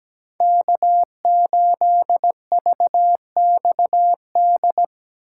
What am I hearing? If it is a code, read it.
Morse code, heard as a keyed tone, about 17 wpm, 700 Hz. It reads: K8VXD
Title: USA Call Signs - 5 Characters - 8x Character Spacing - 17wpm